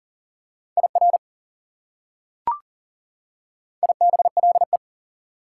Morse code, heard as a keyed tone, about 40 wpm, 700 Hz. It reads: IR IBLE
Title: Prefixes and Suffixes 40wpm